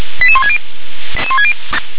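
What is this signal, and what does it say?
Five-tone selective call (ZVEI), heard as quick runs of five tones, tones of about 80 ms. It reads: 88160
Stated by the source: ZVEI five-tone call 88173 received by SDR.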